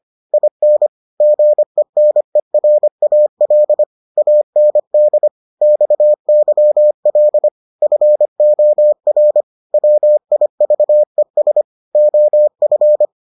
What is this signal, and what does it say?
Morse code, heard as a keyed tone, about 25 wpm, 600 Hz. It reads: IN GENERAL AND XYL FOR WIVES OF